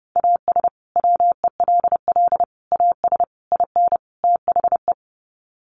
Morse code, heard as a keyed tone, about 30 wpm, 700 Hz. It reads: AS WELL AS IN THE
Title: Sets of 5 Words Encore 30wpm